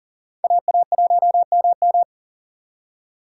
Morse code, heard as a keyed tone, about 51 wpm, 700 Hz. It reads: AA1MM